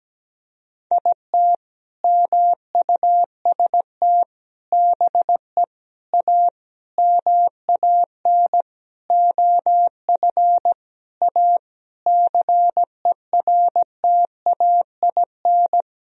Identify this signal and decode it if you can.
Morse code, heard as a keyed tone, about 17 wpm, 700 Hz. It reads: IT MUST BE A MAN OF A CERTAIN
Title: Sentences from Top 1000 Words - Review Part 3 - 17wpm